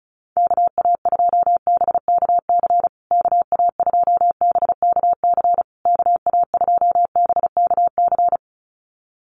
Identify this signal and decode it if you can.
Morse code, heard as a keyed tone, about 34 wpm, 700 Hz. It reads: KA2BKC KA2BKC KA2BKC
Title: USA Call Signs Repeated 3x 35wpm